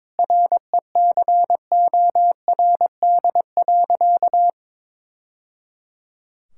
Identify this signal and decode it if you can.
Morse code, heard as a keyed tone, about 22 wpm, 700 Hz. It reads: RECORD.
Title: Sentences from Top 1000 Words - Review Part 6 - 22wpm